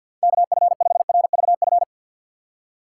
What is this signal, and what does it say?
Morse code, heard as a keyed tone, about 47 wpm, 700 Hz. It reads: KF5RVF
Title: USA Call Signs - 4x Character Spacing - 50wpm